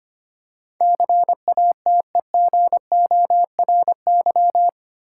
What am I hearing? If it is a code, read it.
Morse code, heard as a keyed tone, about 25 wpm, 700 Hz. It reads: CATEGORY
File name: Eight Letter Words - 25wpm